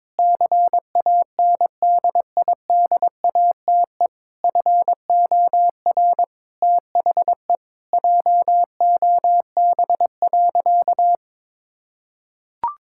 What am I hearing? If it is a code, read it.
Morse code, heard as a keyed tone, about 22 wpm, 700 Hz. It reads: CANDIDATE FOR THE JOB.
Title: Sentences from Top 1600 Words 22wpm